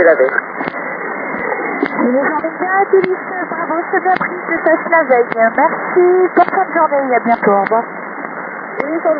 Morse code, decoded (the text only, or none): none